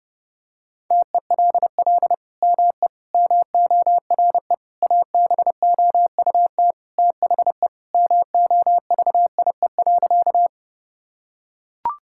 Morse code, heard as a keyed tone, about 32 wpm, 700 Hz. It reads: TELL ME MORE ABOUT THE MOVIE.
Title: Sentences from Top 1000 Words - Review Part 5 - 30wpm